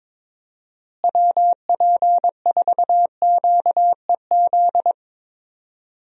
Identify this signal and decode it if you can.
Morse code, heard as a keyed tone, about 21 wpm, 700 Hz. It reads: WP4QEZ
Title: USA Call Signs - 6 Characters - Repeated 2x - 22wpm